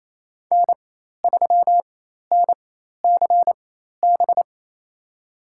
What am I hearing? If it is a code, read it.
Morse code, heard as a keyed tone, about 27 wpm, spaced as for 13 wpm, 700 Hz. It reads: N3NCB
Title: USA Call Signs - 5 Characters - 4x Character Spacing - 28wpm